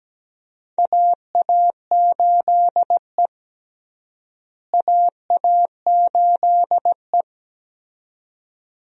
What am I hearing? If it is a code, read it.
Morse code, heard as a keyed tone, about 17 wpm, 700 Hz. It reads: AA8E AA8E
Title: USA Call Signs - 4 Characters - Repeated 2x - 17wpm